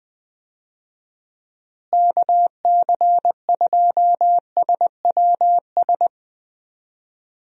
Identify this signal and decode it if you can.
Morse code, heard as a keyed tone, about 20 wpm, 700 Hz. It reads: KC2SWS
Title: USA Call Signs - 6 Characters - Repeated 3x - 20wpm